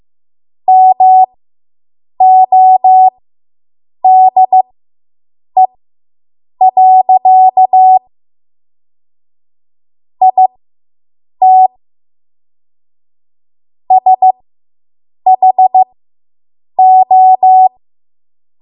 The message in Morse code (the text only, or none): MODE. IT SHO